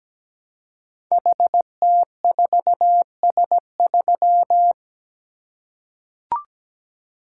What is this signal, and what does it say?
Morse code, heard as a keyed tone, about 17 wpm, 700 Hz. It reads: HT4S3
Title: Code Groups - 5 Letter:Numbers - 17wpm